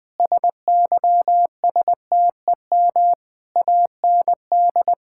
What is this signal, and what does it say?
Morse code, heard as a keyed tone, about 20 wpm, 700 Hz. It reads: SYSTEM AND
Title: Sentences from Top 1000 Words - Review Part 5 - 20wpm